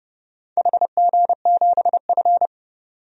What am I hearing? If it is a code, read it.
Morse code, heard as a keyed tone, about 28 wpm, 700 Hz. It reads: HG7F
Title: DXCC-100 Call Signs - 4 Characters - 30wpm